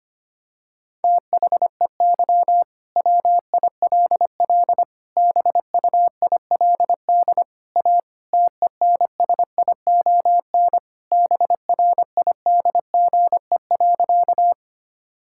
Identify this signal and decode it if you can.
Morse code, heard as a keyed tone, about 25 wpm, 700 Hz. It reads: THEY WILL BUILD A TENSION BRIDGE.
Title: Sentences from Top 1600 Words 25wpm